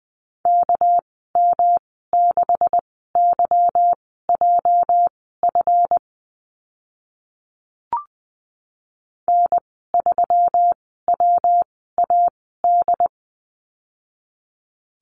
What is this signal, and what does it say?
Morse code, heard as a keyed tone, about 20 wpm, spaced as for 15 wpm, 700 Hz. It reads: KM6YJF N3WAD
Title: 245 Lesson 39g - Letter z - Call Signs 20wpm